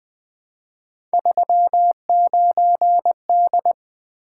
Morse code, heard as a keyed tone, about 20 wpm, 700 Hz. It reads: 39D
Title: Three Letter-Number Combinations - 20wpm